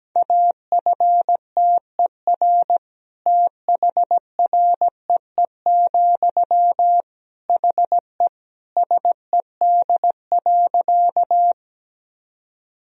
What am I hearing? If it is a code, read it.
Morse code, heard as a keyed tone, about 17 wpm, 700 Hz. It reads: AFTER THREE, HE SED.
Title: Sentences from Top 200 Words and Abbreviations 17wpm